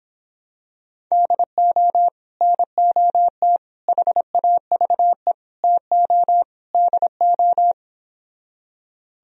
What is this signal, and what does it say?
Morse code, heard as a keyed tone, about 26 wpm, 700 Hz. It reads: DO NOT HAVE TO DO